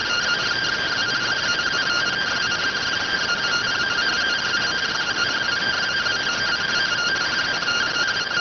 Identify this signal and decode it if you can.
Two-tone teleprinter signal, about 170 Hz shift.